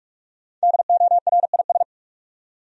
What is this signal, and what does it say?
Morse code, heard as a keyed tone, about 47 wpm, 700 Hz. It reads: DORIS
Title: USA Common Names - Max 5 Letters - 45wpm